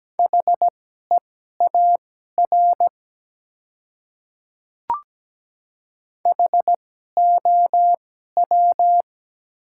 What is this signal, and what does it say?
Morse code, heard as a keyed tone, about 17 wpm, spaced as for 12 wpm, 700 Hz. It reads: HEAR HOW
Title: 2x Character Spacing - Top 300 Words - 17wpm